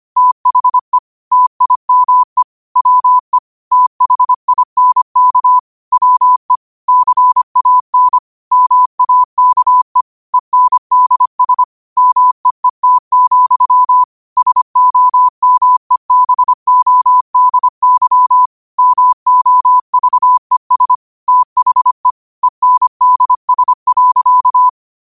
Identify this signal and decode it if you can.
Morse code, heard as a keyed tone, about 25 wpm, 1000 Hz. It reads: THE TIME WE THINK WE CAN MAKE ENDS MEET, SOMEBODY MOVES THE ENDS.
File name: qotdfiles / 25wpm